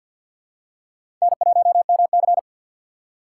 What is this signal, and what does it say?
Morse code, heard as a keyed tone, about 50 wpm, 700 Hz. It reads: N1MC
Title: USA Call Signs - 4 Characters - 6x Character Spacing - 50wpm